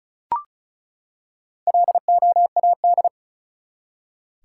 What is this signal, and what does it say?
Morse code, heard as a keyed tone, about 36 wpm, 700 Hz. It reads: LOAD
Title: Four Letter Words - 35wpm